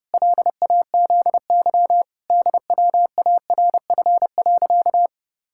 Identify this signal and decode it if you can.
Morse code, heard as a keyed tone, about 32 wpm, 700 Hz. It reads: LAZY DWARF.
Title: Pangram Challenge v2 - 30wpm